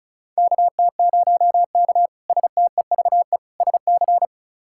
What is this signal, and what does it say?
Morse code, heard as a keyed tone, about 36 wpm, 700 Hz. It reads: KT0K STEVE SC
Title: K1USN SST QSO Practice - 35wpm